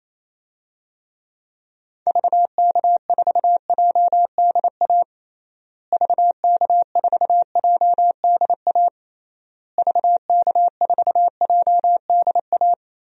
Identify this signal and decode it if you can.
Morse code, heard as a keyed tone, about 28 wpm, 700 Hz. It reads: VK4JDA VK4JDA VK4JDA
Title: DXCC-100 Call Signs - 6 Characters - Repeated 3x - 28wpm